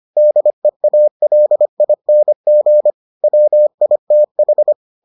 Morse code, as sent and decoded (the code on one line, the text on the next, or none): -.. .- .-.. .. -. --. / .-- .. - ....
DEALING WITH